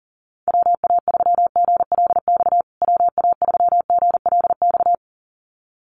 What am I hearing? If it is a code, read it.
Morse code, heard as a keyed tone, about 40 wpm, 700 Hz. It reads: WA3GLX WA3GLX
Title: USA Call Signs Repeated 2x 40wpm